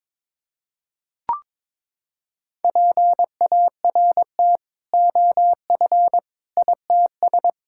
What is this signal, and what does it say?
Morse code, heard as a keyed tone, about 22 wpm, 700 Hz. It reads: PART OF ITS
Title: Sentences from Top 1600 Words 22wpm